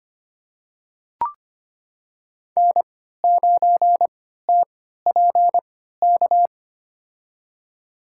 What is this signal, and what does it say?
Morse code, heard as a keyed tone, about 25 wpm, spaced as for 14 wpm, 700 Hz. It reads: N9TPK
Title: USA Call Signs - 5 Characters - 3x Character Spacing - 25wpm